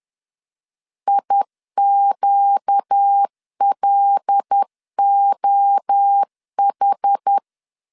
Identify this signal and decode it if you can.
Morse code, heard as a keyed tone, about 10 wpm, 800 Hz. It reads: IQLOH
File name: LEZIONI / 8 YJ45 / 8 Y 10wpm 2615